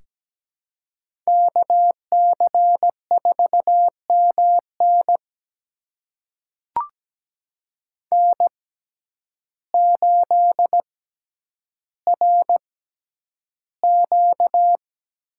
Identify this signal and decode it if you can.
Morse code, heard as a keyed tone, about 17 wpm, spaced as for 9 wpm, 700 Hz. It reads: KC4MN N8RQ
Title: USA Call Signs - 5 Characters - 6x Character Spacing - 17wpm